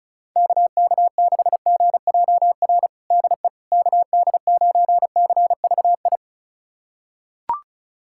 Morse code, heard as a keyed tone, about 34 wpm, 690 Hz. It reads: KK6GJR DE KD9CVI
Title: USA Callsign DE Callsign - 6 Characters - 35wpm